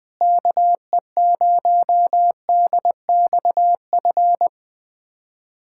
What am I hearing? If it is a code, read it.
Morse code, heard as a keyed tone, about 21 wpm, 700 Hz. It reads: KE0DXF
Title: USA Callsign DE Callsign - 6 Characters - 20wpm